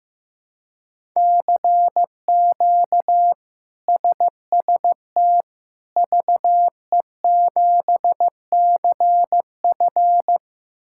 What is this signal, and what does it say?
Morse code, heard as a keyed tone, about 15 wpm, 700 Hz. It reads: CQ SST VE7CF
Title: K1USN SST QSO Practice - 15wpm